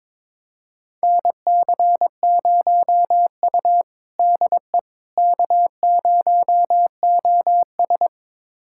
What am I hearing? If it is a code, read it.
Morse code, heard as a keyed tone, about 22 wpm, 700 Hz. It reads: NC0U DE K0OS
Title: USA Callsign DE Callsign - 4 Characters - 22wpm